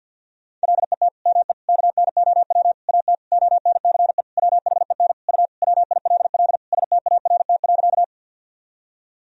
Morse code, heard as a keyed tone, about 50 wpm, 700 Hz. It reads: LET ME KNOW AT ONCE WHEN U WILL START.
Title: Sentences from Top 300 Words and Abbreviations 50wpm